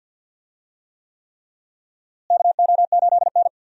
Morse code, heard as a keyed tone, about 50 wpm, 700 Hz. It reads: KO8N
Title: USA Call Signs - 4 Characters - Repeated 3x - 50wpm